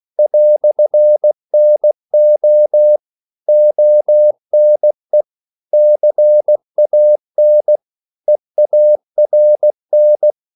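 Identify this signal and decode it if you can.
Morse code, heard as a keyed tone, about 16 wpm, 600 Hz. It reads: "NO ONE CAN EARN
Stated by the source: English quotations sent in CW as practice at 16 words per minute.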